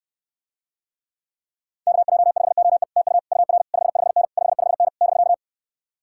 Morse code, heard as a keyed tone, about 69 wpm, 700 Hz. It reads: KJ5OE TU UR 55N 55N <BK>